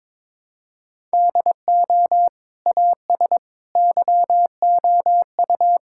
Morse code, heard as a keyed tone, about 22 wpm, 700 Hz. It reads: DO AS YOU
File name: practice / 22wpm / Sentences with 2-Word Phrases - 22wpm